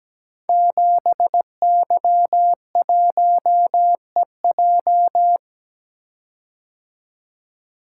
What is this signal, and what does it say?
Morse code, heard as a keyed tone, about 17 wpm, 700 Hz. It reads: 7Y1EJ
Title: Code Groups - 5 Letter:Numbers - 17wpm